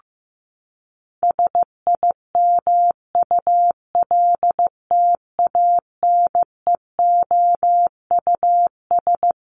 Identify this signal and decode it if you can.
Morse code, heard as a keyed tone, about 15 wpm, 700 Hz. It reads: SIMULTANEOUS
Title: LY Suffix - Top 100 Words with Base Words 15wpm